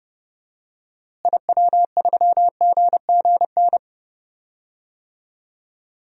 Morse code, heard as a keyed tone, about 30 wpm, 700 Hz. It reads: IW3GGN
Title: DXCC-100 Call Signs - 6 Characters - Repeated 2x - 30wpm